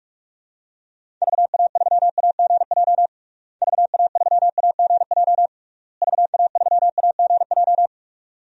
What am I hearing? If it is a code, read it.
Morse code, heard as a keyed tone, about 41 wpm, 700 Hz. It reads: VA3AGJ VA3AGJ VA3AGJ
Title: DXCC-100 Call Signs - 6 Characters - Repeated 3x - 45wpm